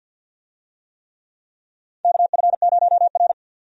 Morse code, heard as a keyed tone, about 47 wpm, 700 Hz. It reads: KF0R
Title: USA Call Signs - 4 Characters - 50wpm